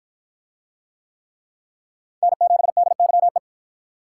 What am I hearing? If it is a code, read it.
Morse code, heard as a keyed tone, about 53 wpm, 700 Hz. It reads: N7DYE